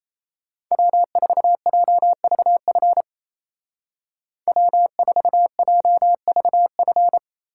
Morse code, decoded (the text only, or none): W4JVF W4JVF